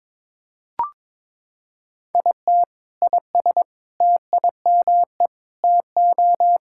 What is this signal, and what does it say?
Morse code, heard as a keyed tone, about 21 wpm, 700 Hz. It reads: IT IS TIME TO
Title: Sentences from Top 1600 Words 22wpm